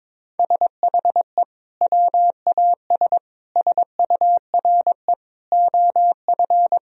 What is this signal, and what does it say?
Morse code, heard as a keyed tone, about 22 wpm, 700 Hz. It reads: SHE WAS SURE OF
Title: Sentences from Top 1000 Words - Review Part 1 - 22wpm